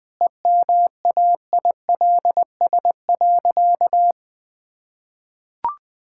Morse code, heard as a keyed tone, about 21 wpm, 700 Hz. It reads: EMAILS.
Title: Easy Sentences with Hints - 20wpm